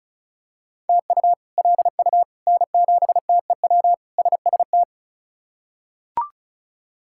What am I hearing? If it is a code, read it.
Morse code, heard as a keyed tone, about 34 wpm, 700 Hz. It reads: TU LU N7TEW SST
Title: K1USN SST QSO Practice - 35wpm